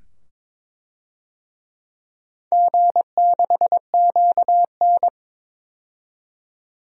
Morse code, heard as a keyed tone, about 21 wpm, 700 Hz. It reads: G6QN
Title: DXCC-100 Call Signs - 4 Characters - 22wpm